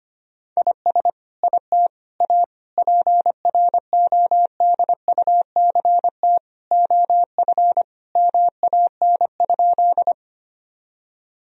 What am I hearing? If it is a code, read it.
Morse code, heard as a keyed tone, about 25 wpm, 700 Hz. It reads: IS IT A PRODUCT OF MAN?
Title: Sentences from Top 1000 Words - Review Part 2 - 25wpm